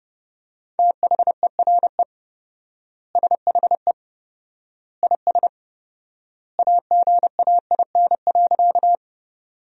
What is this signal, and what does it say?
Morse code, heard as a keyed tone, about 32 wpm, 700 Hz. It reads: THERE SHE IS AGAIN.
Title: 4x Word-Spacing - Sentences from Top 200 Words - 30wpm